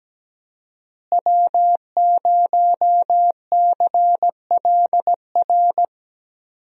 Morse code, heard as a keyed tone, about 17 wpm, 700 Hz. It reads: W0CLR
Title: USA Call Signs - 5 Characters - Repeated 3x - 17wpm